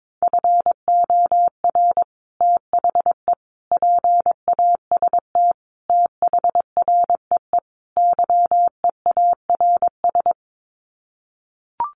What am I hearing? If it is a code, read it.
Morse code, heard as a keyed tone, about 22 wpm, 700 Hz. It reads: FOR THE PAST THREE YEARS
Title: Sets of 5 Words Encore2 22wpm